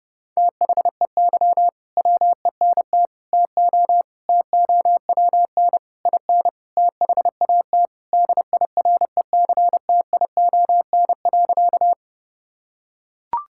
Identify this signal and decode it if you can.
Morse code, heard as a keyed tone, about 29 wpm, 700 Hz. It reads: THEY WENT TO TOWN IN THAT DIRECTION.
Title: Sentences from Top 1000 Words - Review Part 5 - 30wpm